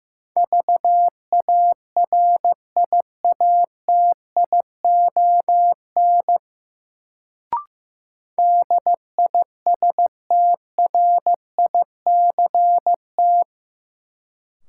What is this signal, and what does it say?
Morse code, heard as a keyed tone, about 16 wpm, 700 Hz. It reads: VARIATION DISTRICT
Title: Sentences from Top 1600 Words - New Word Drill Alt - 15wpm